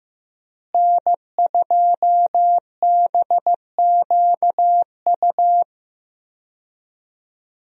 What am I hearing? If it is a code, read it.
Morse code, heard as a keyed tone, about 15 wpm, 700 Hz. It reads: N2BQU
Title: USA Call Signs - 5 Characters - Repeated 3x - 15wpm